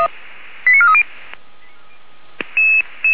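Five-tone selective call (ZVEI), heard as quick runs of five tones, tones of about 70 ms. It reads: 89430
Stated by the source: Five-tone ZVEI selective call 89336, received by SDR.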